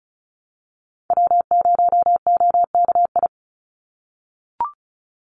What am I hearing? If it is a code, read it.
Morse code, heard as a keyed tone, about 34 wpm, 700 Hz. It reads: W0OKI
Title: USA Call Signs 35wpm